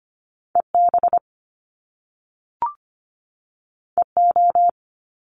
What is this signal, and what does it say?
Morse code, heard as a keyed tone, about 25 wpm, 700 Hz. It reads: EB EO
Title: Top Two Letter Combinations in Words 25wpm